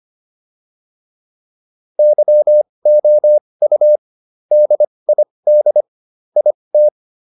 Morse code, heard as a keyed tone, about 25 wpm, spaced as for 20 wpm, 600 Hz. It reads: YOU DID IT